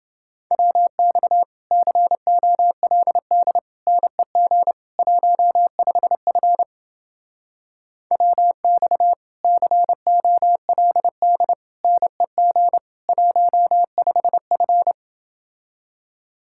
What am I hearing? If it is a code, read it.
Morse code, heard as a keyed tone, about 30 wpm, 700 Hz. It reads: WX COLD NEG 15F WX COLD NEG 15F